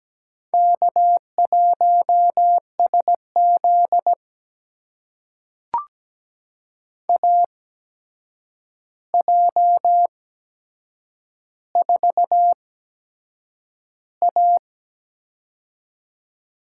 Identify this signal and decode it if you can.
Morse code, heard as a keyed tone, about 17 wpm, spaced as for 7 wpm, 700 Hz. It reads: K1SZ AJ4A
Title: USA Call Signs - 4 Characters - 8x Character Spacing - 17wpm